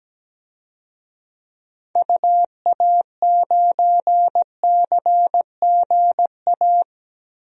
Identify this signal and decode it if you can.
Morse code, heard as a keyed tone, about 17 wpm, 700 Hz. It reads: UA9CGA
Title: DXCC-100 Call Signs - 6 Characters - Repeated 3x - 17wpm